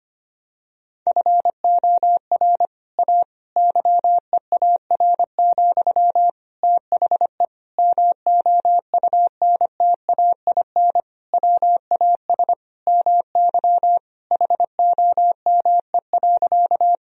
Morse code, heard as a keyed tone, about 25 wpm, 700 Hz. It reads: FOR A YEAR, THE MOUNTAIN WAS MY HOME.
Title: Sentences from Top 1000 Words - Review Part 1 - 25wpm